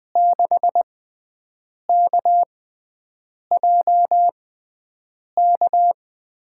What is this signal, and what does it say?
Morse code, heard as a keyed tone, about 20 wpm, spaced as for 7 wpm, 700 Hz. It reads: 6KJK
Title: USA Call Signs - 6 Characters - 6x Character Spacing - 20wpm